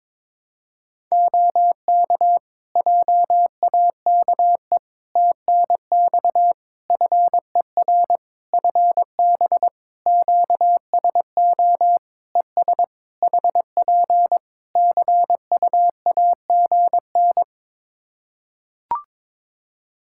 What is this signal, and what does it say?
Morse code, heard as a keyed tone, about 22 wpm, 700 Hz. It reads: OK JAKE TNX FER FB QSO ES HP CUAGN
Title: Practice QSO - Exchange 4 - Ending - 22wpm